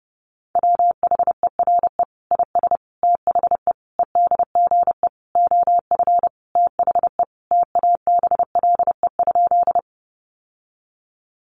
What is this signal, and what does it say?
Morse code, heard as a keyed tone, about 30 wpm, 700 Hz. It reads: WHERE IS THE EDGE OF THE TABLE?
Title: Sentences from Top 1300 Words 30wpm